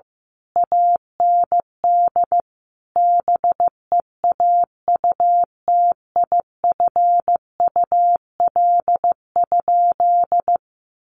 Morse code, heard as a keyed tone, about 15 wpm, 700 Hz. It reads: AND BEAUTIFUL?
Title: Sentences from Top 1200 Words 15wpm